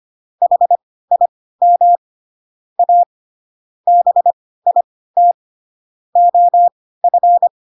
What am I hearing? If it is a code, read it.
Morse code, heard as a keyed tone, about 25 wpm, spaced as for 16 wpm, 700 Hz. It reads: HIM A BIT OF